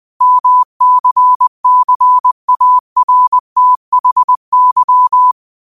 Morse code, heard as a keyed tone, about 20 wpm, 1000 Hz. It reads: MCCARTHY
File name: qotdfiles / 20wpm